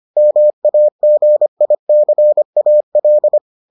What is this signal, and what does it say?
Morse code, heard as a keyed tone, about 25 wpm, 600 Hz. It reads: MAGICAL